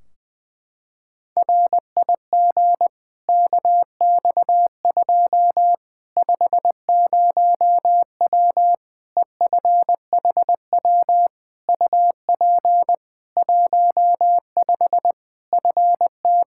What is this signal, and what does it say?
Morse code, heard as a keyed tone, about 20 wpm, 700 Hz. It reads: RIG KX2 50W EFHW UP 15 FT
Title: K1USN SST QSO Practice 20wpm